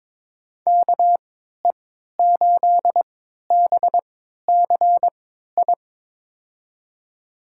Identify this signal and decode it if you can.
Morse code, heard as a keyed tone, about 22 wpm, spaced as for 13 wpm, 700 Hz. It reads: KE8BCI
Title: USA Call Signs - 6 Characters - 3x Character Spacing - 22wpm